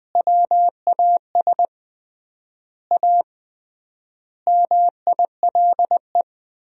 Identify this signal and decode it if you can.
Morse code, heard as a keyed tone, about 21 wpm, 700 Hz. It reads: WAS A MILE
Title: 3x Word-Spacing - Sentences from Top 500 Words - 20wpm